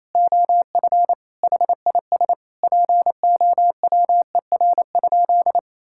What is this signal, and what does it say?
Morse code, heard as a keyed tone, about 28 wpm, 700 Hz. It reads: OF HIS POWER?
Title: Sentences from Top 1000 Words - Review Part 5 - 28wpm